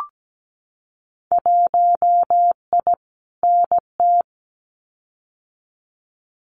Morse code, heard as a keyed tone, about 17 wpm, 700 Hz. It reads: 1I NT
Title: Winter Field Day Exchanges Encore 17wpm